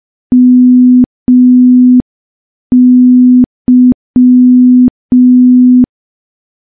Morse code, heard as a keyed tone, about 5 wpm, 250 Hz. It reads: MY
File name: morse / morse-training-05WPM